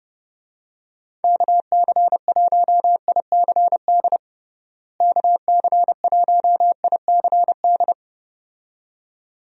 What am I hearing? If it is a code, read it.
Morse code, heard as a keyed tone, about 30 wpm, 700 Hz. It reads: KC1ICD KC1ICD
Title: USA Call Signs - 6 Characters - Repeated 2x - 30wpm